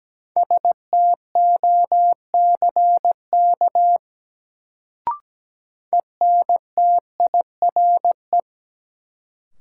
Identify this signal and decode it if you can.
Morse code, heard as a keyed tone, about 17 wpm, 700 Hz. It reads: STOCK ENTIRE
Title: Sentences from Top 800 Words - New Word Drill - 17wpm